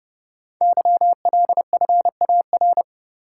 Morse code, heard as a keyed tone, about 30 wpm, 700 Hz. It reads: YLFAR
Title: Code Groups - 5 Letters - 30wpm